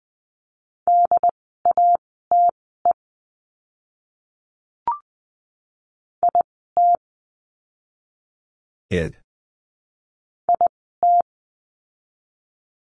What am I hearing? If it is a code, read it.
Morse code, heard as a keyed tone, about 20 wpm, spaced as for 14 wpm, 700 Hz. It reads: DATE IT IT
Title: 068 Lesson 12f - Letter d - Words 20wpm